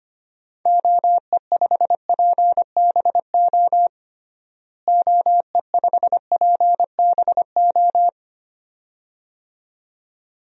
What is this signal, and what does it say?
Morse code, heard as a keyed tone, about 25 wpm, 700 Hz. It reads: OE5PBO OE5PBO
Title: DXCC-100 Call Signs - 6 Characters - Repeated 2x - 25wpm